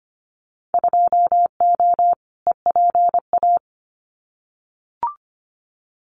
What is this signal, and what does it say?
Morse code, heard as a keyed tone, about 25 wpm, 700 Hz. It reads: 2O EPA
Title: Winter Field Day Exchanges Encore 25wpm